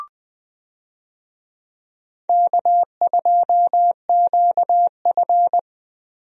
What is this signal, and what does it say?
Morse code, heard as a keyed tone, about 20 wpm, 700 Hz. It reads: K2QF
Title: USA Call Signs - 4 Characters - Repeated 2x - 20wpm